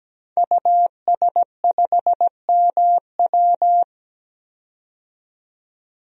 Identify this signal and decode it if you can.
Morse code, heard as a keyed tone, about 17 wpm, 700 Hz. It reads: US5MW
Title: DXCC-100 Call Signs - 5 Characters - 17wpm